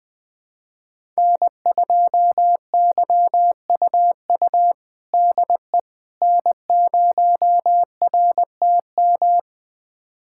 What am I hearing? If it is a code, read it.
Morse code, heard as a keyed tone, about 20 wpm, 700 Hz. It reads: N2YUU DE N0RTM
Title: USA Callsign DE Callsign - 5 Characters - 20wpm